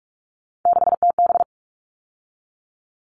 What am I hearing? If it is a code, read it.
Morse code, heard as a keyed tone, about 44 wpm, 700 Hz. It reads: 6TB
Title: All Cut Numbers - 3-Digit 45wpm